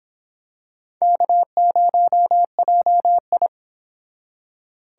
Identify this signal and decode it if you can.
Morse code, heard as a keyed tone, about 25 wpm, 700 Hz. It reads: K0JI